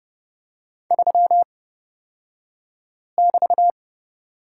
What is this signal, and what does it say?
Morse code, heard as a keyed tone, about 31 wpm, spaced as for 5 wpm, 700 Hz. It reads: 3<BT>
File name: Lesson 36d - BT - Two Characters 30wpm